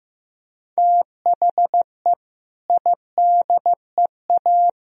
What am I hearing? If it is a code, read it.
Morse code, heard as a keyed tone, about 16 wpm, 700 Hz. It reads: THE IDEA
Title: Speed-Racing - Sets of 3 Words 15wpm